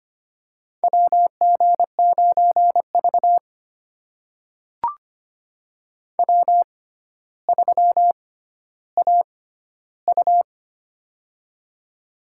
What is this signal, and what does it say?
Morse code, heard as a keyed tone, about 25 wpm, spaced as for 13 wpm, 700 Hz. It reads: WG9V W3AU
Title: USA Call Signs - 4 Characters - 6x Character Spacing - 25wpm